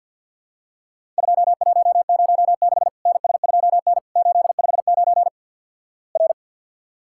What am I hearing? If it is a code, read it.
Morse code, heard as a keyed tone, about 50 wpm, 700 Hz. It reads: 2106 NS2N 859 R
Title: Worked All Europe DX Contest - QTC Exchange with Repeats - 50wpm